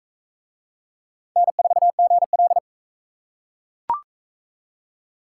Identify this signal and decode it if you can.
Morse code, heard as a keyed tone, about 42 wpm, 700 Hz. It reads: N4GL